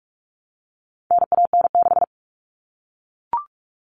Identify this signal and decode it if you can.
Morse code, heard as a keyed tone, about 44 wpm, 700 Hz. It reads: NAN6
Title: All Cut Numbers - 4-Digit 45wpm